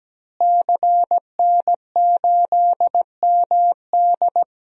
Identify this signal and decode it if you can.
Morse code, heard as a keyed tone, about 17 wpm, 700 Hz. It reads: CN8MD
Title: DXCC-100 Call Signs - 5 Characters - Repeated 2x - 17wpm